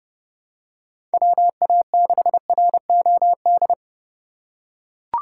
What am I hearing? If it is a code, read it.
Morse code, heard as a keyed tone, about 30 wpm, 700 Hz. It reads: WA6ROD
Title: USA Call Signs - 2x Character Spacing - 30wpm